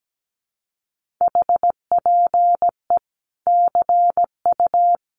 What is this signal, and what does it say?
Morse code, heard as a keyed tone, about 17 wpm, 700 Hz. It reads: HPE CU
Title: Common QSO Phrases 17wpm